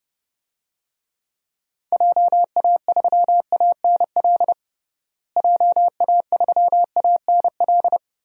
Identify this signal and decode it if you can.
Morse code, heard as a keyed tone, about 32 wpm, 700 Hz. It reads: JA3ANL JA3ANL
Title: DXCC-100 Call Signs - 6 Characters - Repeated 2x - 30wpm